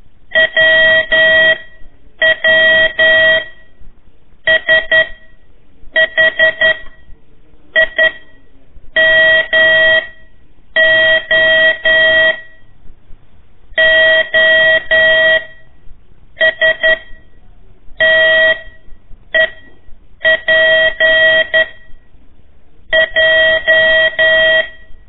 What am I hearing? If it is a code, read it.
Morse code, heard as a keyed tone, about 11 wpm, 1900 Hz. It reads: W W S H I M O O S T E P J
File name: 3 lezione CW 1 AIR